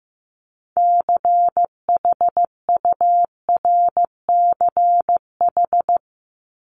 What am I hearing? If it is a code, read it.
Morse code, heard as a keyed tone, about 15 wpm, 700 Hz. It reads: CHURCH
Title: Top 1751-2000 Words 15wpm